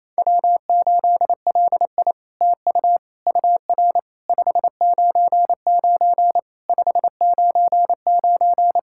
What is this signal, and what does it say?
Morse code, heard as a keyed tone, about 28 wpm, 700 Hz. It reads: W8LI TU UR 599 599
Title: SOTA QSO Practice 28wpm